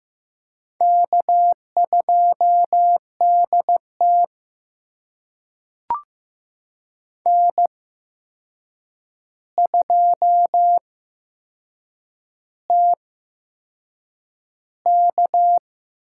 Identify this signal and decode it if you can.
Morse code, heard as a keyed tone, about 15 wpm, spaced as for 6 wpm, 700 Hz. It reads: K2DT N2TK
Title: USA Call Signs - 8x Character Spacing - 15wpm